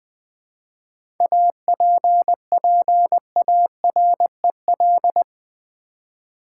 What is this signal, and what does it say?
Morse code, heard as a keyed tone, about 20 wpm, 700 Hz. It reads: APPAREL
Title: 7 Letter Words - First 3 Letters Given - 20wpm